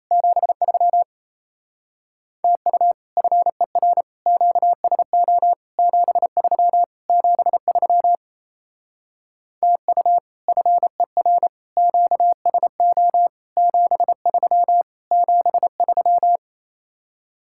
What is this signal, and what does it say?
Morse code, heard as a keyed tone, about 34 wpm, 700 Hz. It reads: 73 TU FER QSO 73 73 TU FER QSO 73 73
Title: Speed-Racing Common QSO Phrases Encore - 28wpm